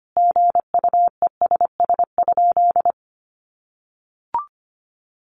Morse code, heard as a keyed tone, about 25 wpm, 700 Hz. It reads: GUESS?
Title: Sentences from Top 1500 Words 25wpm